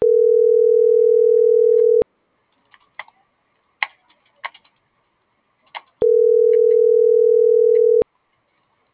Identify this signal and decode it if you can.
Ringback tone (North America), 440 + 480 Hz.